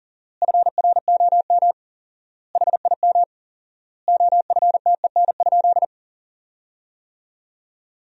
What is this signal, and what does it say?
Morse code, heard as a keyed tone, about 40 wpm, 700 Hz. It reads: FROM HIM OFTEN?
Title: 4x Word-Spacing - Sentences from Top 400 Words - 40wpm